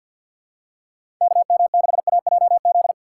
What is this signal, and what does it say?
Morse code, heard as a keyed tone, about 53 wpm, 700 Hz. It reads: KM6AJZ
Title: USA Call Signs - 6 Characters - 8x Character Spacing - 50wpm